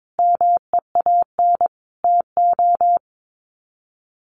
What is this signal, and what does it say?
Morse code, heard as a keyed tone, about 22 wpm, 700 Hz. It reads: MEAN TO
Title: Sets of 5 Words Encore2 22wpm